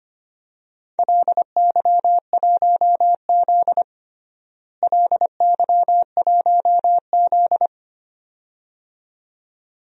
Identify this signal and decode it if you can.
Morse code, heard as a keyed tone, about 25 wpm, 700 Hz. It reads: LY1Z LY1Z
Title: DXCC-100 Call Signs - 4 Characters - Repeated 2x - 25wpm